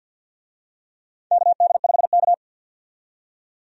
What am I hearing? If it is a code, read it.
Morse code, heard as a keyed tone, about 53 wpm, 700 Hz. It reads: KD5K
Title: USA Call Signs - 4 Characters - 50wpm